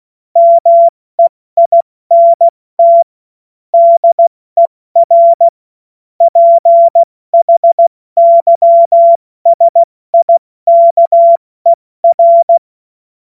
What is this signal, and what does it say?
Morse code, heard as a keyed tone, about 16 wpm, 680 Hz. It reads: MEINT DER PHYSIKER
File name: Morsekurs / Witze / Witz 10.Oktober 2023 70 F80